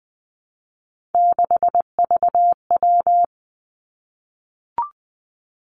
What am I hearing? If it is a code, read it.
Morse code, heard as a keyed tone, about 20 wpm, 700 Hz. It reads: 6VW
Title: Three Challenging Letter-Number Combinations 20wpm